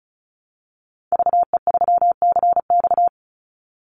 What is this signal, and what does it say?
Morse code, heard as a keyed tone, about 34 wpm, 700 Hz. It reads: VE3CX
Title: Call Signs Repeated 2x - 35wpm